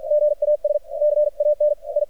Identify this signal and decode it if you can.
Morse code, heard as a keyed tone, about 53 wpm, spaced as for 46 wpm, 600 Hz, with radio noise. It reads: GAS CAN I
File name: archived / CRSnet